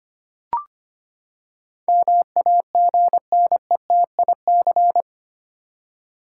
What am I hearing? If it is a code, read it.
Morse code, heard as a keyed tone, about 25 wpm, 700 Hz. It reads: MAGNETIC
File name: Eight Letter Words - 25wpm